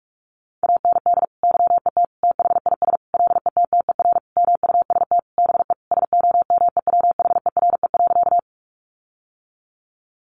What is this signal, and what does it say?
Morse code, heard as a keyed tone, about 44 wpm, 700 Hz. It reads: AND YET THIS LETTER MUST BE SOMEWHERE.
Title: Sentences from Top 1400 Words 45wpm